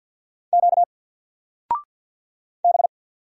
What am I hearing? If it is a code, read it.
Morse code, heard as a keyed tone, about 48 wpm, 700 Hz. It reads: Q B
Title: Single Character (Letter:Number:Punctuation:Prosign) Alt 50wpm